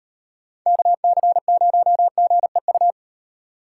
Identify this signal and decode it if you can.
Morse code, heard as a keyed tone, about 36 wpm, 700 Hz. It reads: KC0GEU